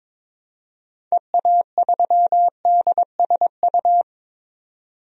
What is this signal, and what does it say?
Morse code, heard as a keyed tone, about 22 wpm, 700 Hz. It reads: EA3DSU
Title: DXCC-100 Call Signs - 6 Characters - 22wpm